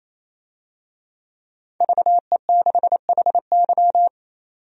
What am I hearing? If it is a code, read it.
Morse code, heard as a keyed tone, about 28 wpm, 700 Hz. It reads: VE6HY
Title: DXCC-100 Call Signs - 5 Characters - 28wpm